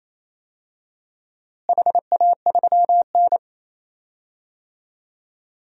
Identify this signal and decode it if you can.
Morse code, heard as a keyed tone, about 28 wpm, 700 Hz. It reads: HA3N
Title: DXCC-100 Call Signs - 4 Characters - Repeated 3x - 28wpm